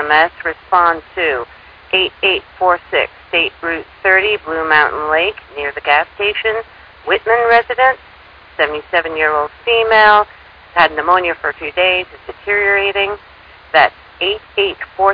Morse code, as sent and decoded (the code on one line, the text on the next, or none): none